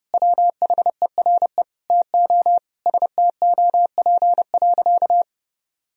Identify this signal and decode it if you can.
Morse code, heard as a keyed tone, about 30 wpm, 700 Hz. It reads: WHERE TO STOP.
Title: Sentences from Top 1000 Words - Review Part 2 - 30wpm